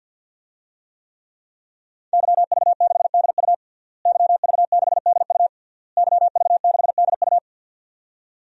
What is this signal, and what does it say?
Morse code, heard as a keyed tone, about 52 wpm, 700 Hz. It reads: YV6DU YV6DU YV6DU
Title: DXCC-100 Call Signs - 5 Characters - Repeated 3x - 50wpm